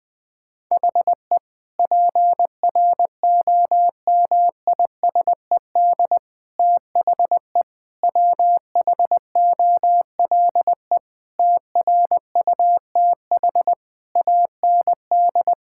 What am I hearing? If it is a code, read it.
Morse code, heard as a keyed tone, about 20 wpm, 700 Hz. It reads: HE PROMISED THE WHOLE TRUTH AND
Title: Sentences with 8-Word Phrases - 20wpm